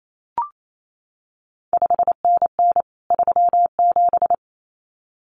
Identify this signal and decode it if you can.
Morse code, heard as a keyed tone, about 28 wpm, 700 Hz. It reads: 5NN 37
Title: CQ WW DX Contest Exchange Encore - 28wpm